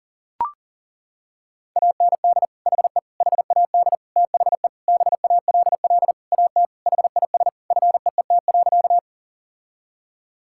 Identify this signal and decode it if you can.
Morse code, heard as a keyed tone, about 40 wpm, 700 Hz. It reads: AND HE HAD THE BALL AT HIS FEET.
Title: Sentences from Top 1000 Words - Review Part 6 - 40wpm